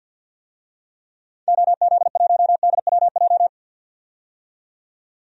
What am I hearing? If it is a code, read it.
Morse code, heard as a keyed tone, about 46 wpm, 700 Hz. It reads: OZ1DWJ
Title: DXCC-100 Call Signs - 6 Characters - 50wpm